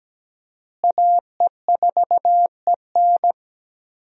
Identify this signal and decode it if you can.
Morse code, heard as a keyed tone, about 17 wpm, 700 Hz. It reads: AE4EN
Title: USA Call Signs - 5 Characters - Repeated 3x - 17wpm